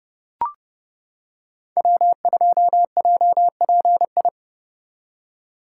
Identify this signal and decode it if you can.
Morse code, heard as a keyed tone, about 30 wpm, 700 Hz. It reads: W2JPI